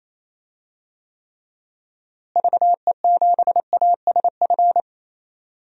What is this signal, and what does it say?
Morse code, heard as a keyed tone, about 28 wpm, 700 Hz. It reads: VE7ASF
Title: DXCC-100 Call Signs - 6 Characters - Repeated 3x - 28wpm